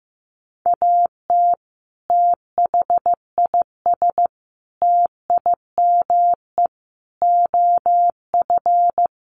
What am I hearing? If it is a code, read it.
Morse code, heard as a keyed tone, about 15 wpm, 700 Hz. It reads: AT THIS TIME OF
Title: Sets of 5 Words Encore2 15wpm